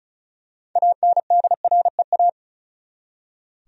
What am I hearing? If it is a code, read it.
Morse code, heard as a keyed tone, about 34 wpm, 700 Hz. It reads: ANDREA
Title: USA Common Names - 35wpm